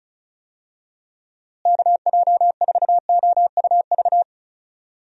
Morse code, heard as a keyed tone, about 34 wpm, 700 Hz. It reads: KJ4OUV
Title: USA Call Signs - 6 Characters - Repeated 3x - 35wpm